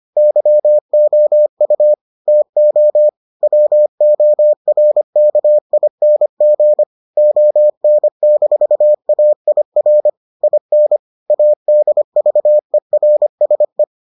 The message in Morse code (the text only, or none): YOU TO WORKING ON-AIR IN ADVERSE